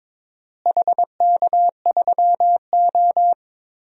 Morse code, heard as a keyed tone, about 23 wpm, 700 Hz. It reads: HK3O
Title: DXCC-100 Call Signs - 4 Characters - Repeated 3x - 22wpm